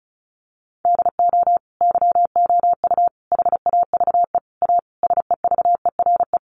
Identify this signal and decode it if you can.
Morse code, heard as a keyed tone, about 34 wpm, 700 Hz. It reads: DO YOU HAVE A SEVERE
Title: Sentences from Top 1400 Words 35wpm